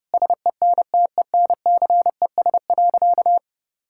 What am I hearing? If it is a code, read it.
Morse code, heard as a keyed tone, about 30 wpm, 700 Hz. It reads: SENTENCES.
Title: Pangram Challenge v2 - 30wpm